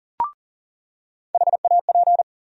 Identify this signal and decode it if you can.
Morse code, heard as a keyed tone, about 40 wpm, 700 Hz. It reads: HAP
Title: USA Call Sign Postfixes - 3 Letters 40wpm